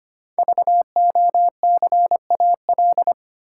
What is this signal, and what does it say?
Morse code, heard as a keyed tone, about 25 wpm, 700 Hz. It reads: VOCAL
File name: 5 Letter Words - First 3 Letters Given - 25wpm